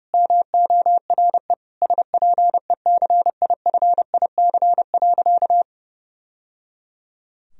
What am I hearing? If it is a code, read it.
Morse code, heard as a keyed tone, about 32 wpm, 700 Hz. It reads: MORE SPECIFIC.
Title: Sentences from Top 1000 Words - Review Part 3 - 30wpm